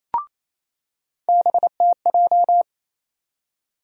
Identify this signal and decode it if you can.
Morse code, heard as a keyed tone, about 28 wpm, 700 Hz. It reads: BTJ
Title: USA Call Sign Postfixes - 3 Letters 28wpm